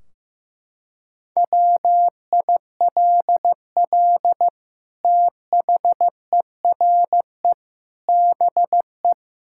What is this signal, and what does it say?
Morse code, heard as a keyed tone, about 15 wpm, 700 Hz. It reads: WILL THERE BE
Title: Sentences from Top 1600 Words 15wpm